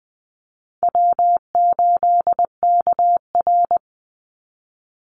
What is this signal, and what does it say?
Morse code, heard as a keyed tone, about 20 wpm, 700 Hz. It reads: W8KR
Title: Easy Call Signs - 20wpm